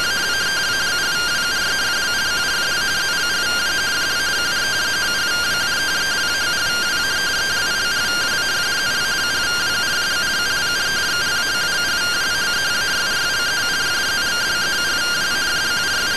Two-tone teleprinter signal, about 170 Hz shift.